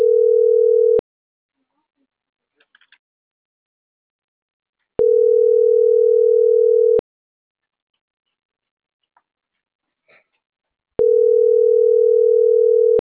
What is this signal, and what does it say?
Ringback tone (North America), 440 + 480 Hz.